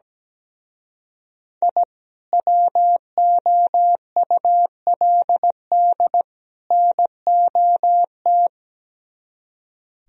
Morse code, heard as a keyed tone, about 17 wpm, 700 Hz. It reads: I WOULD NOT
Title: Speed-Racing - Sets of 3 Words 17wpm